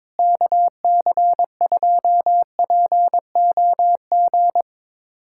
Morse code, heard as a keyed tone, about 22 wpm, 700 Hz. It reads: KC2POG
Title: USA Callsign DE Callsign 22wpm